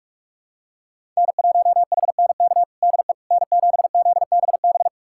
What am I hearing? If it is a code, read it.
Morse code, heard as a keyed tone, about 46 wpm, 700 Hz. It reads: N1HNK DE N7ZBB
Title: USA Callsign DE Callsign - 5 Characters - 45wpm